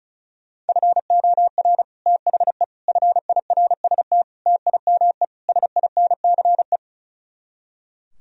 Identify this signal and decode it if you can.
Morse code, heard as a keyed tone, about 46 wpm, 700 Hz. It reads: FOR THE FIRST TIME SINCE